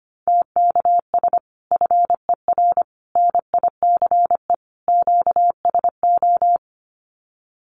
Morse code, heard as a keyed tone, about 25 wpm, 700 Hz. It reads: TKS FER NICE QSO
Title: Common QSO Phrases 25wpm